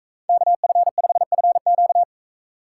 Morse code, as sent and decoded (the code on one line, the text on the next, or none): -.- ..-. ..... ..-. --.-
KF5FQ